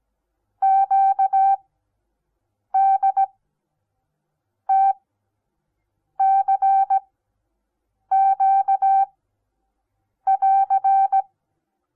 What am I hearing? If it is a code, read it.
Morse code, heard as a keyed tone, about 17 wpm, spaced as for 6 wpm, 780 Hz. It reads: QDTCQ <AR>
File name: Onlinelehrgang / morsen / DJ4UF Morsen08